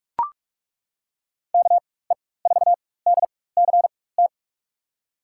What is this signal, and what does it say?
Morse code, heard as a keyed tone, about 42 wpm, spaced as for 21 wpm, 700 Hz. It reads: KE4DCT
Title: USA Call Signs - 6 Characters - 4x Character Spacing - 45wpm